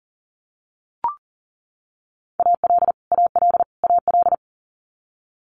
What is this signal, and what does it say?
Morse code, heard as a keyed tone, about 40 wpm, 700 Hz. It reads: AL AL AL
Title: Multi-letter Phonemes Repeated 3x 40wpm